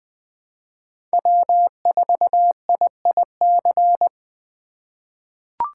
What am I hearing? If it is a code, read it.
Morse code, heard as a keyed tone, about 20 wpm, 700 Hz. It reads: W4IIC
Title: K1USN SST QSO Practice - 20wpm